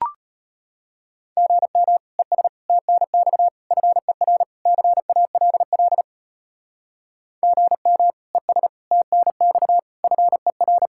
Morse code, heard as a keyed tone, about 31 wpm, 700 Hz. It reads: GM ES TNX FER CALL GM ES TNX FER
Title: Speed-Racing Common QSO Phrases - 25wpm